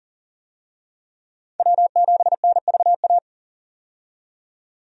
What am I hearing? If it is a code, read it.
Morse code, heard as a keyed tone, about 40 wpm, 700 Hz. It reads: W7NVA
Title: USA Call Signs - 5 Characters - 40wpm